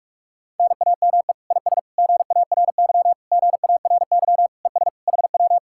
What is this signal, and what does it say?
Morse code, heard as a keyed tone, about 40 wpm, 700 Hz. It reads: NAME IS GARY GARY ES HW